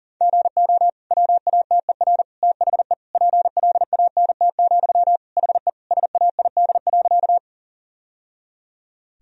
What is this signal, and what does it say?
Morse code, heard as a keyed tone, about 40 wpm, 700 Hz. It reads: GO WATER THE PLANT, HE SAID.
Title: Sentences from Top 1000 Words - Review Part 1 - 40wpm